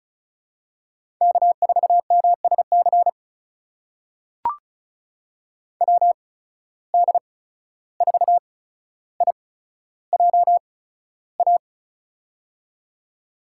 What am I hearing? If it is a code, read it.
Morse code, heard as a keyed tone, about 36 wpm, spaced as for 14 wpm, 700 Hz. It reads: K4MSC WD4IJA
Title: USA Call Signs - 8x Character Spacing - 35wpm